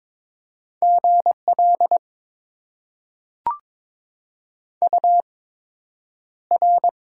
Morse code, heard as a keyed tone, about 22 wpm, spaced as for 9 wpm, 700 Hz. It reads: GL UR
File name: Top 90 QSO Elements - 8x Character Spacing - 22wpm